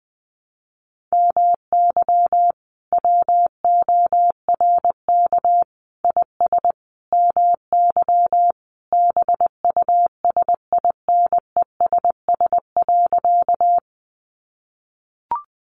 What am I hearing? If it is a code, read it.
Morse code, heard as a keyed tone, about 20 wpm, 700 Hz. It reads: MY WORK IS MY BUSINESS.
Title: Sentences from Top 600 Words 20wpm